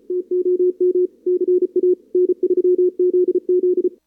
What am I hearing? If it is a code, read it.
Morse code, heard as a keyed tone, about 34 wpm, 360 Hz. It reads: TOM CA N3ZZ
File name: CWSprint / Feb2022 / k5tr-feb-na-sprint-cw-k5ot -0000